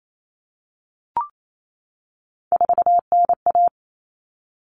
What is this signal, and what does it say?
Morse code, heard as a keyed tone, about 28 wpm, 700 Hz. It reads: 4NA
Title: Common Cut Numbers - 3-Digit 28wpm